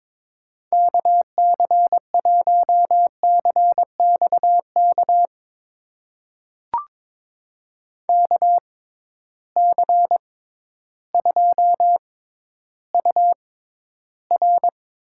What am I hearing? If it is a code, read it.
Morse code, heard as a keyed tone, about 22 wpm, spaced as for 12 wpm, 700 Hz. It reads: KC1CXK KC2UR
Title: USA Call Signs - 6 Characters - 6x Character Spacing - 22wpm